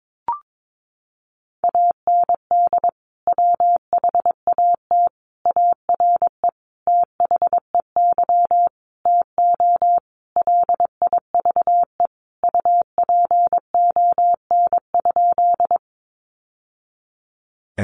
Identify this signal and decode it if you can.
Morse code, heard as a keyed tone, about 22 wpm, 700 Hz. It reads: AND WHAT ARE THEY TO LIVE UPON?
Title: Sentences from Top 1100 Words 22wpm